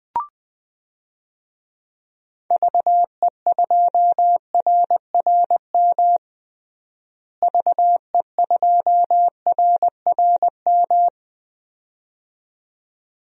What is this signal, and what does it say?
Morse code, heard as a keyed tone, about 21 wpm, 700 Hz. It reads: VE2RRM VE2RRM
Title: DXCC-100 Call Signs - 6 Characters - Repeated 2x - 20wpm